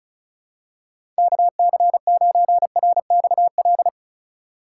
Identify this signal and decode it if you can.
Morse code, heard as a keyed tone, about 34 wpm, 700 Hz. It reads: KC9RXL
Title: USA Call Signs - 6 Characters - 2x Character Spacing - 35wpm